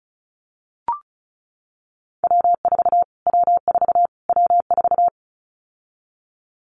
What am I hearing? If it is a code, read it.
Morse code, heard as a keyed tone, about 34 wpm, 700 Hz. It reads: W4 W4 W4
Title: Callsign Prefixes for Top 25 DXCC - Repeated 3x - 35wpm